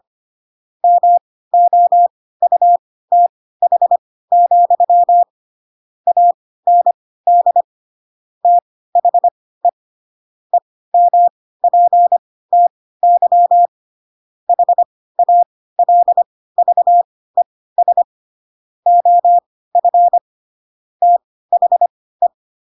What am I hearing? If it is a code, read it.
Morse code, heard as a keyed tone, about 25 wpm, spaced as for 16 wpm, 700 Hz. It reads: MOUTH, AND THE EMPTY HALVES OF THE